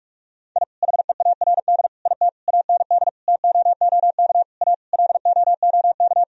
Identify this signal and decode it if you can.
Morse code, heard as a keyed tone, about 42 wpm, 700 Hz. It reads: I HEARD IT AND TOOK A LOOK
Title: Sentences from Top 1000 Words - Review Part 1 - 45wpm